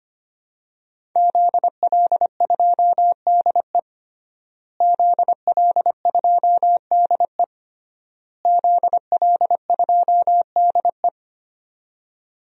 Morse code, heard as a keyed tone, about 25 wpm, 700 Hz. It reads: ZL2DE ZL2DE ZL2DE
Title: DXCC-100 Call Signs - 5 Characters - Repeated 3x - 25wpm